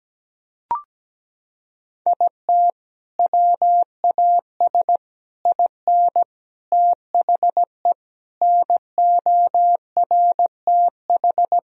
Morse code, heard as a keyed tone, about 17 wpm, 700 Hz. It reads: IT WAS IN THE NORTH
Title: Sentences from Top 1000 Words - Review Part 1 - 17wpm